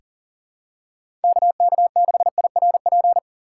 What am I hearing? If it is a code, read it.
Morse code, heard as a keyed tone, about 39 wpm, 700 Hz. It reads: KK6IRP
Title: USA Call Signs - 6 Characters - 2x Character Spacing - 40wpm